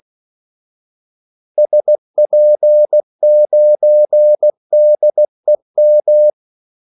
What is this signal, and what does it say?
Morse code, heard as a keyed tone, about 16 wpm, 600 Hz. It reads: SP9DEM
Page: Kaldesignaler 16 wpm